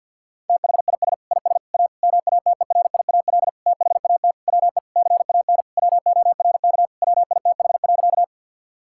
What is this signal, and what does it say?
Morse code, heard as a keyed tone, about 50 wpm, 700 Hz. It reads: THIS IS A MATERIAL THAT WE CAN WORK WITH.
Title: Sentences from Top 1000 Words - Review Part 5 - 50wpm